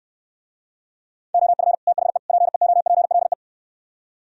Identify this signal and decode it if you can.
Morse code, heard as a keyed tone, about 69 wpm, 700 Hz. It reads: OF THE PEOPLE